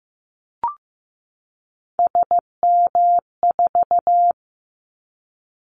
Morse code, heard as a keyed tone, about 15 wpm, 700 Hz. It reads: SM4
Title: Callsign Prefixes for Top 25 DXCC - 15wpm